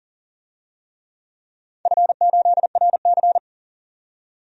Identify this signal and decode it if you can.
Morse code, heard as a keyed tone, about 37 wpm, 700 Hz. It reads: F8RC
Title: DXCC-100 Call Signs - 4 Characters - 40wpm